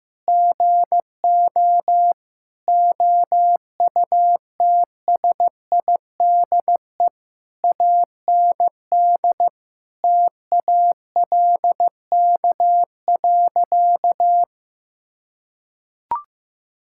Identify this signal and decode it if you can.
Morse code, heard as a keyed tone, about 15 wpm, 700 Hz. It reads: GO OUTSIDE AND TALK.
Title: Sentences from Top 1000 Words - Review Part 6 - 15wpm